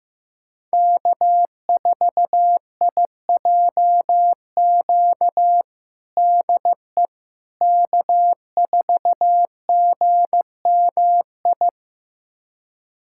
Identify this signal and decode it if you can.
Morse code, heard as a keyed tone, about 15 wpm, 700 Hz. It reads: K4IJQ DE K4GMI
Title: USA Callsign DE Callsign 15wpm